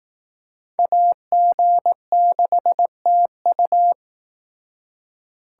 Morse code, heard as a keyed tone, about 18 wpm, 700 Hz. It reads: AG6TU
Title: Speed Racing - USA 2x2 Call Signs 15wpm